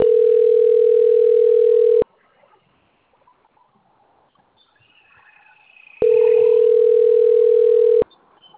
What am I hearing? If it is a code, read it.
Ringback tone (North America), 440 + 480 Hz.